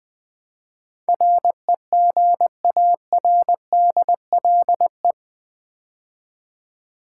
Morse code, heard as a keyed tone, about 20 wpm, 700 Hz. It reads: REGARDLE
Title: Build A Word - 4 Letters At A Time - 10-Letter Words - 20wpm